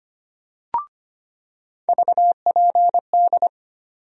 Morse code, heard as a keyed tone, about 25 wpm, 700 Hz. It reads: VPD
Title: USA Call Sign Postfixes - 3 Letters 25wpm